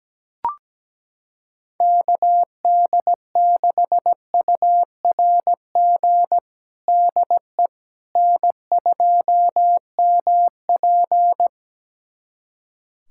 Morse code, heard as a keyed tone, about 17 wpm, 700 Hz. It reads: KD6URG DE N2MP
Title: USA Callsign DE Callsign 17wpm